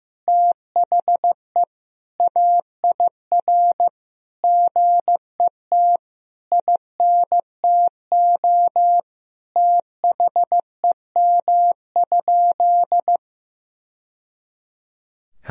Morse code, heard as a keyed tone, about 15 wpm, 700 Hz. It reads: THE AIR GET INTO THEM?
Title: Sentences from Top 1000 Words - Review Part 1 - 15wpm